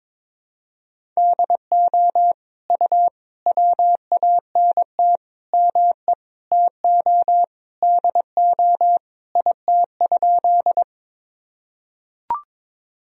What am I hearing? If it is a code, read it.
Morse code, heard as a keyed tone, about 22 wpm, 700 Hz. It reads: DO U WANT ME TO DO IT?
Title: Sentences from Top 200 Words and Abbreviations 22wpm